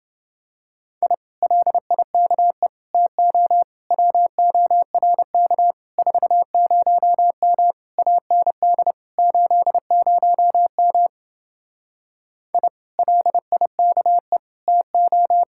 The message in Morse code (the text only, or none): I LIKE TO WORK 40M AND 80M I LIKE TO